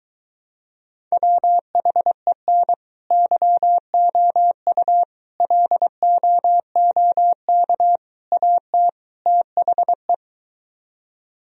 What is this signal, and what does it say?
Morse code, heard as a keyed tone, about 24 wpm, 700 Hz. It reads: WHEN YOU LOOK AT THE